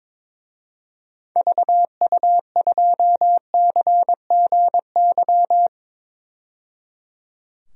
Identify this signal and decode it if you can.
Morse code, heard as a keyed tone, about 22 wpm, 700 Hz. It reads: VU2CGY
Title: DXCC-100 Call Signs - 6 Characters - 22wpm